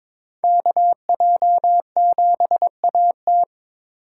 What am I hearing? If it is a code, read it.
Morse code, heard as a keyed tone, about 26 wpm, 700 Hz. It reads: KJ7AT
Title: Speed Racing - USA 2x2 Call Signs 22wpm